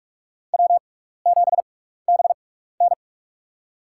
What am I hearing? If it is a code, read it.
Morse code, heard as a keyed tone, about 46 wpm, spaced as for 16 wpm, 700 Hz. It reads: W7BN
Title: USA Call Signs - 4 Characters - 6x Character Spacing - 45wpm